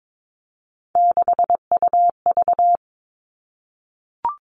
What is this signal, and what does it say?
Morse code, heard as a keyed tone, about 22 wpm, 700 Hz. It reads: 6UV
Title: All Cut Numbers - 3-Digit 22wpm